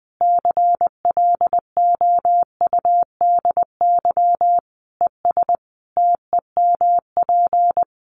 Morse code, heard as a keyed tone, about 20 wpm, 700 Hz. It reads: CLOUDY ES TEMP
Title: Common QSO Phrases 20wpm